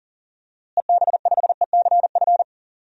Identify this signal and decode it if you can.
Morse code, heard as a keyed tone, about 38 wpm, 700 Hz. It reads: EB5ECF
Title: DXCC-100 Call Signs - 6 Characters - Repeated 2x - 40wpm